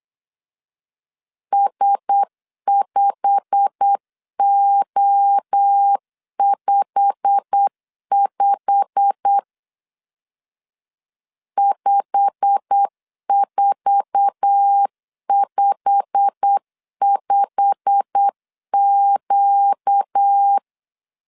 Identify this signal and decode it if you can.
Morse code, heard as a keyed tone, about 8 wpm, 800 Hz. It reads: S5O55 5455Q
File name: LEZIONI / 8 YJ45 / 8 YJ45 8wpm 3214